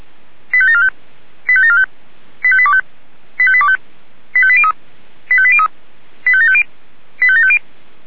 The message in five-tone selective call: 85745 85745 85825 85825 85803 85803 85750 85750

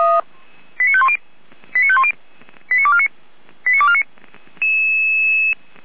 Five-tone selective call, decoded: 89510 89510 89249 89249

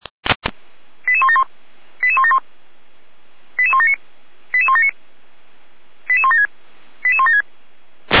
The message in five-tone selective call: 88171 88171 88179 88179 88176 88176

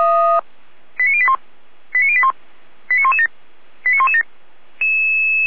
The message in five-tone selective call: 89081 89081 89117 89117

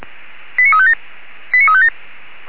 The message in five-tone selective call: 89357 89357

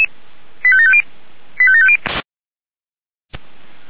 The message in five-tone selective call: 85750 85750

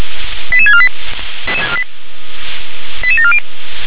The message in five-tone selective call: 88648 88740

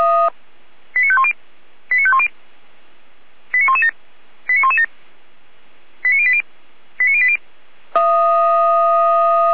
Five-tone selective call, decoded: 89510 89510 89117 89117 89080 89080

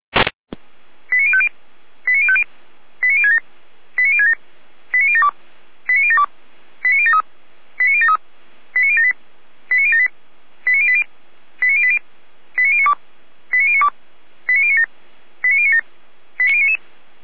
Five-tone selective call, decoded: 89050 89050 89067 89067 89072 89072 89073 89073 89078 89078 89080 89080 89092 89092 89097 89097 89099

